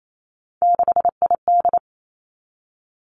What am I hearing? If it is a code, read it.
Morse code, heard as a keyed tone, about 28 wpm, 700 Hz. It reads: 6ID
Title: Three Challenging Letter-Number Combinations 28wpm